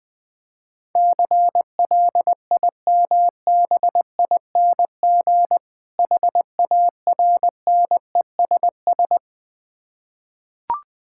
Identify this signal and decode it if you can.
Morse code, heard as a keyed tone, about 20 wpm, 700 Hz. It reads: CLIMBING HARNESS
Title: Difficult Words with Hints - 20wpm